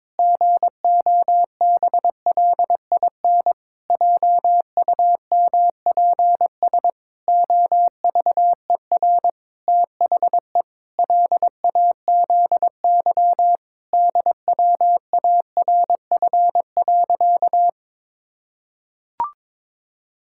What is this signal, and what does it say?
Morse code, heard as a keyed tone, about 22 wpm, 700 Hz. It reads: GOBLIN JUMPS OVER THE LAZY DWARF.
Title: Pangram Challenge v2 - 22wpm